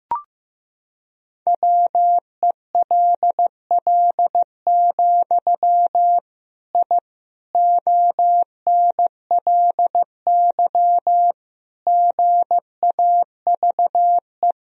Morse code, heard as a keyed tone, about 15 wpm, 700 Hz. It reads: WELL, I ONLY GAVE
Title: Sentences from Top 1000 Words - Review Part 2 - 15wpm